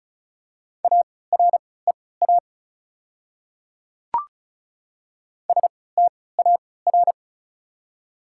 Morse code, heard as a keyed tone, about 34 wpm, spaced as for 20 wpm, 700 Hz. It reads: AREA STAR
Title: 3x Character Spacing - Top 500 Words - 35wpm